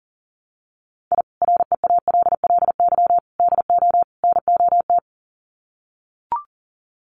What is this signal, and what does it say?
Morse code, heard as a keyed tone, about 40 wpm, 700 Hz. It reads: I REALLY DO NOT
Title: Sets of 4 Words 40wpm